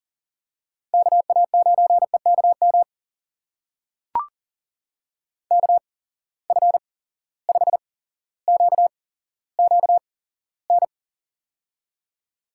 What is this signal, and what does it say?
Morse code, heard as a keyed tone, about 40 wpm, spaced as for 17 wpm, 700 Hz. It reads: KA9EKM KF5QQN
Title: USA Call Signs - 6 Characters - 8x Character Spacing - 40wpm